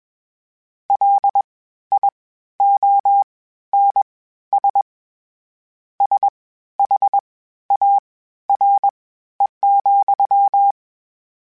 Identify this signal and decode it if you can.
Morse code, heard as a keyed tone, about 21 wpm, spaced as for 13 wpm, 800 Hz. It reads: LIONS SHARE,
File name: Misc / Misc-mp3s / WordPairs sp @20